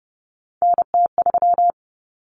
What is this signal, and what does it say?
Morse code, heard as a keyed tone, about 30 wpm, 700 Hz. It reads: NT3
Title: Common Cut Numbers - 3-Digit 30wpm